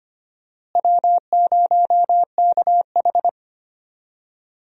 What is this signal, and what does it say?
Morse code, heard as a keyed tone, about 25 wpm, 700 Hz. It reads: W0KH
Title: USA Call Signs - 4 Characters - Repeated 2x - 25wpm